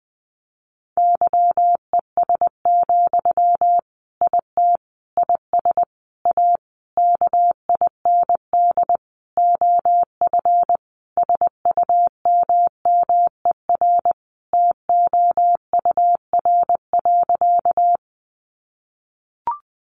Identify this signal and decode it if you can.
Morse code, heard as a keyed tone, about 20 wpm, 700 Hz. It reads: YES, IT IS A KIND OF SUMMER TOUR.
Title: Sentences from Top 1400 Words 20wpm